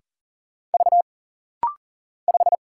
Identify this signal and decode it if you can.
Morse code, heard as a keyed tone, about 40 wpm, 700 Hz. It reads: V 5
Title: Single Letter-Number - Rapid-Fire 40wpm